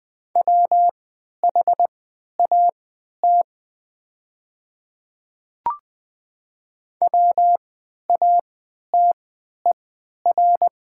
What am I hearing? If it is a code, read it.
Morse code, heard as a keyed tone, about 20 wpm, spaced as for 11 wpm, 700 Hz. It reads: WHAT WATER
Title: 3x Character Spacing - Top 200 Words - 20wpm